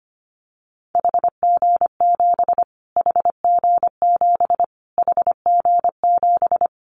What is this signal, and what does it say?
Morse code, heard as a keyed tone, about 25 wpm, 700 Hz. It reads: HG7 HG7 HG7
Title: Callsign Prefixes for Top 25 DXCC - Repeated 3x - 25wpm